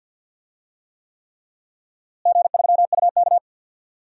Morse code, heard as a keyed tone, about 51 wpm, 700 Hz. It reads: G3UK
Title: DXCC-100 Call Signs - 4 Characters - Repeated 3x - 50wpm